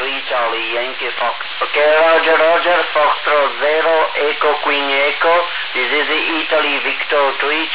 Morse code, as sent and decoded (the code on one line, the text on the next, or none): none